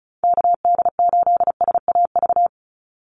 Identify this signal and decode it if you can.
Morse code, heard as a keyed tone, about 34 wpm, 700 Hz. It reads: KD8SAV
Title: Call Signs - 35wpm